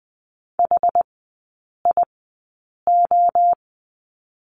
Morse code, heard as a keyed tone, about 20 wpm, spaced as for 8 wpm, 690 Hz. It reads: HIO
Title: 081 Lesson 14e - Number 2 - Three Characters 20wpm